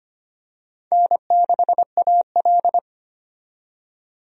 Morse code, heard as a keyed tone, about 25 wpm, 700 Hz. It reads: N6AL
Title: USA Call Signs - 4 Characters - 25wpm